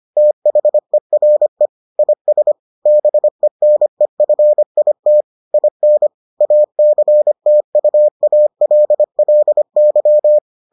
Morse code, heard as a keyed tone, about 25 wpm, 600 Hz. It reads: THERE IS BENEFIT IN ACTUALLY